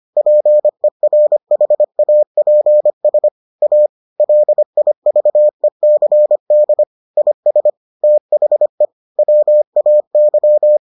Morse code, heard as a keyed tone, about 25 wpm, 600 Hz. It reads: PERHAPS A LIVECD IS THE WAY